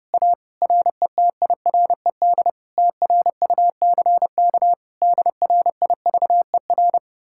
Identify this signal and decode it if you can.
Morse code, heard as a keyed tone, about 33 wpm, 700 Hz. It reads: A RETIRED TRUCK DRIVER